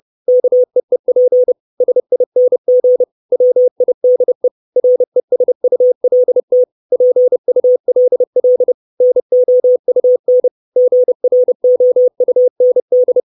Morse code, heard as a keyed tone, about 30 wpm, 500 Hz. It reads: KEEP SING WIDE RESULT PULL NOUN GROUND
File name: 632 Words 30WPM